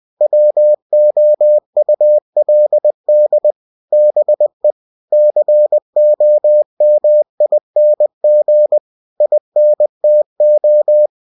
Morse code, heard as a keyed tone, about 20 wpm, 600 Hz. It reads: WOULD BE COMING INTO
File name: cw-books / winnie-the-pooh / winnie-20wpm-page-0091